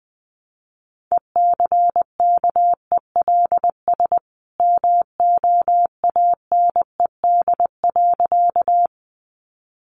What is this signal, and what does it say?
Morse code, heard as a keyed tone, about 20 wpm, 700 Hz. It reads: ECKELS MOANED.
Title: A Sound Of Thunder 20wpm